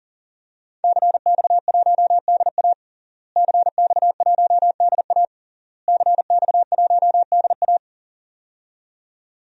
Morse code, heard as a keyed tone, about 37 wpm, 700 Hz. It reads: CX1DA CX1DA CX1DA
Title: DXCC-100 Call Signs - 5 Characters - Repeated 3x - 40wpm